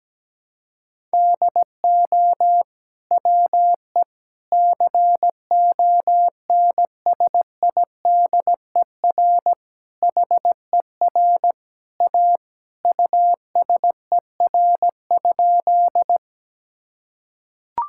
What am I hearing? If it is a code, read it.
Morse code, heard as a keyed tone, about 17 wpm, 700 Hz. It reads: DO WE CONSIDER HER A USER?
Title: Sentences from Top 1000 Words - Review Part 5 - 17wpm